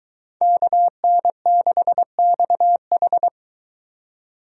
Morse code, heard as a keyed tone, about 22 wpm, 700 Hz. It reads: KN6XH